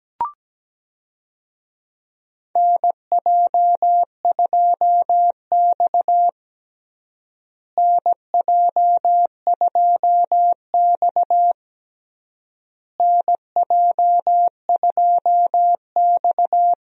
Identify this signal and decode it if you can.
Morse code, heard as a keyed tone, about 17 wpm, 700 Hz. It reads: NJ2X NJ2X NJ2X
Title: USA Call Signs - 4 Characters - Repeated 3x - 17wpm